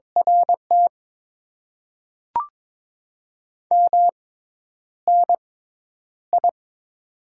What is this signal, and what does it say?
Morse code, heard as a keyed tone, about 22 wpm, spaced as for 10 wpm, 700 Hz. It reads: RT MNI
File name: Top 90 QSO Elements - 6x Character Spacing - 22wpm